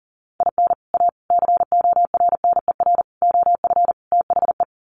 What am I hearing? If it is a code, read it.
Morse code, heard as a keyed tone, about 40 wpm, 700 Hz. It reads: IN A CORNER OF THE
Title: Sets of 5 Words Encore2 40wpm